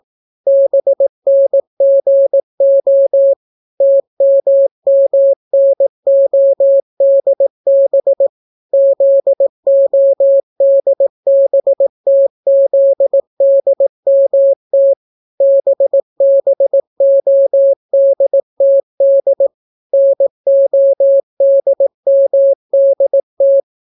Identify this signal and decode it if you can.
Morse code, heard as a keyed tone, about 18 wpm, 550 Hz. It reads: BNGO TMMNODB ZODBTZDMT BBODTD NODMDT